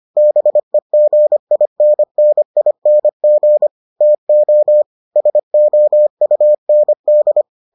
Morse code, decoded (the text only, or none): BEGINNING TO SOUND